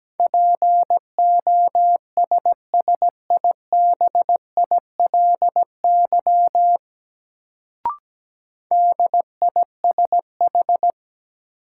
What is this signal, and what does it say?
Morse code, heard as a keyed tone, about 17 wpm, 700 Hz. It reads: POSSIBILY DISH
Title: Sentences from Top 1300 Words - New Word Drill Alt - 17wpm